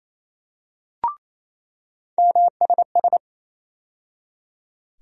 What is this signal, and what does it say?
Morse code, heard as a keyed tone, about 28 wpm, 700 Hz. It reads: MSS
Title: USA Call Sign Postfixes - 3 Letters 28wpm